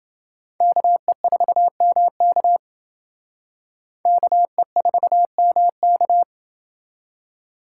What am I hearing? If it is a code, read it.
Morse code, heard as a keyed tone, about 28 wpm, 700 Hz. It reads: KE4MK KE4MK